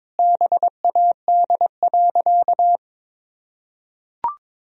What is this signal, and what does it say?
Morse code, heard as a keyed tone, about 22 wpm, 700 Hz. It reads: BAD.
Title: Sentences from Top 1000 Words - Review Part 2 - 22wpm